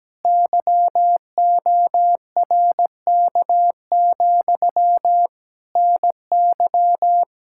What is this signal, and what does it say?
Morse code, heard as a keyed tone, about 17 wpm, 700 Hz. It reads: YORK, NY
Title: Speed-Racing Common QSO Phrases - 17wpm